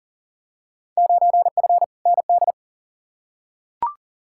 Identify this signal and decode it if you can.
Morse code, heard as a keyed tone, about 40 wpm, 700 Hz. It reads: 9F ND
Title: Field Day Exchanges Encore - 40wpm